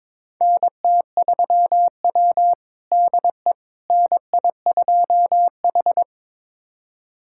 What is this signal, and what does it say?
Morse code, heard as a keyed tone, about 22 wpm, 700 Hz. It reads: NT3W DE NI2H
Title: USA Callsign DE Callsign - 4 Characters - 22wpm